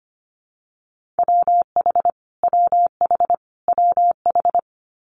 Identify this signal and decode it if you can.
Morse code, heard as a keyed tone, about 25 wpm, 700 Hz. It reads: WH WH WH
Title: Multi-letter Phonemes Repeated 3x 25wpm